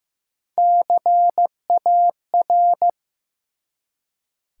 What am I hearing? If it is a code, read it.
Morse code, heard as a keyed tone, about 15 wpm, 700 Hz. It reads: CAR
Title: Three Letter Words - 15wpm